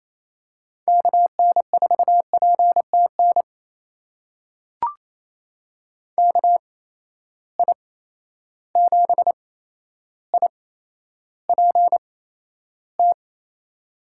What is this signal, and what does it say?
Morse code, heard as a keyed tone, about 28 wpm, spaced as for 12 wpm, 700 Hz. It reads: KN4PTN KI7IPT
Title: USA Call Signs - 6 Characters - 8x Character Spacing - 28wpm